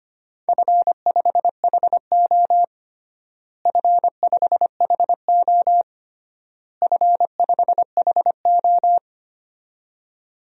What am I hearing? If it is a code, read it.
Morse code, heard as a keyed tone, about 25 wpm, 700 Hz. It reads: F5HO F5HO F5HO
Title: DXCC-100 Call Signs - 4 Characters - Repeated 3x - 25wpm